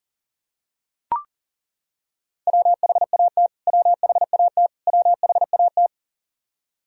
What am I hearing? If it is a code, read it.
Morse code, heard as a keyed tone, about 38 wpm, 700 Hz. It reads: WHAT WHAT WHAT
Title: Top 100 Words Repeated 40wpm